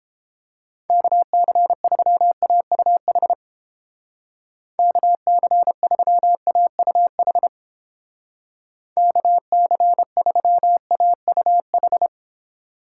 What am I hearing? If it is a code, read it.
Morse code, heard as a keyed tone, about 31 wpm, 700 Hz. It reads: KC3AUH KC3AUH KC3AUH